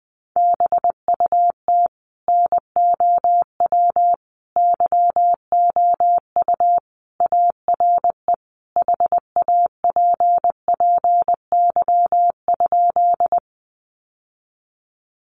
Morse code, heard as a keyed tone, about 20 wpm, 700 Hz. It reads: BUT NOW YOU ARE HAPPY?
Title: Sentences from Top 1000 Words 20wpm